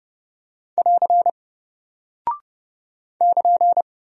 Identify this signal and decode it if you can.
Morse code, heard as a keyed tone, about 28 wpm, 700 Hz. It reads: <AR> <KN>
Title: Single Character (Letter:Number:Punctuation:Prosign) Alt 30wpm